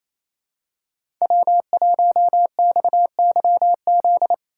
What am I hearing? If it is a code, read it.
Morse code, heard as a keyed tone, about 28 wpm, 700 Hz. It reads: W1XYZ
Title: USA Call Signs - 5 Characters - 2x Character Spacing - 28wpm